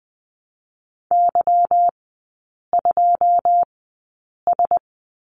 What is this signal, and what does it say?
Morse code, heard as a keyed tone, about 20 wpm, spaced as for 8 wpm, 700 Hz. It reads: Y2S
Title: 145 Lesson 24e - Letter y - Three Characters 20wpm